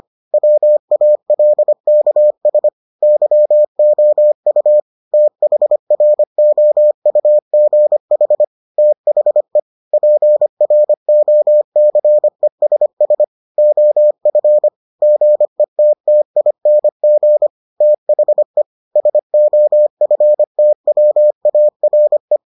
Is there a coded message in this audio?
Yes: Morse code from 0.3 to 22.4 s, heard as a keyed tone, about 25 wpm, 600 Hz. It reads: WALKS YOU THROUGH THE PROCESS OF GETTING THE SOFTWARE